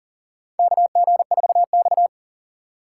Morse code, heard as a keyed tone, about 40 wpm, 700 Hz. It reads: KG4X
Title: USA Call Signs - 2x Character Spacing - 40wpm